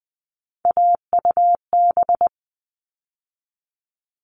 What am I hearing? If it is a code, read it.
Morse code, heard as a keyed tone, about 20 wpm, 700 Hz. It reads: AUB
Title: All Cut Numbers - 3-Digit 20wpm